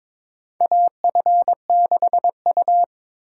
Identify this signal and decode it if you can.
Morse code, heard as a keyed tone, about 22 wpm, 700 Hz. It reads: AF6U
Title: USA Call Signs - 4 Characters - 8x Character Spacing - 22wpm